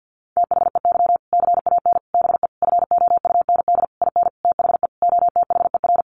Morse code, heard as a keyed tone, about 50 wpm, 700 Hz. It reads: THEY CAN BE FOUND IN THE OTHER
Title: Sentences from Top 1500 Words 50wpm